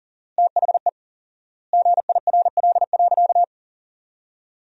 Morse code, heard as a keyed tone, about 40 wpm, 700 Hz. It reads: THE GIRL.
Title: 4x Word-Spacing - Sentences from Top 400 Words - 40wpm